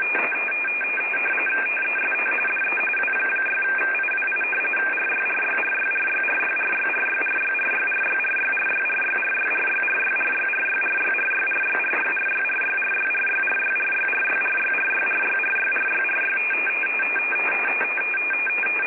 RTTY, about 45.45 baud. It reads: TEST DE W6IWI RYRYRYRYRYRYRYRYRYRYRYRYRYRYRYRYRYRYRYRYRYRYRYRYRYRYRYRYRYRYRYRYRYRYRY
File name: rtty / websdr recording 14089.1kHz